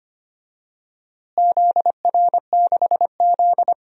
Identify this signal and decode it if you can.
Morse code, heard as a keyed tone, about 25 wpm, 700 Hz. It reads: ZR6Z
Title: DXCC-100 Call Signs - 4 Characters - Repeated 3x - 25wpm